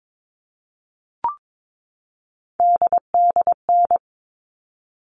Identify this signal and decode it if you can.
Morse code, heard as a keyed tone, about 22 wpm, 700 Hz. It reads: DDN
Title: All Cut Numbers - 3-Digit 22wpm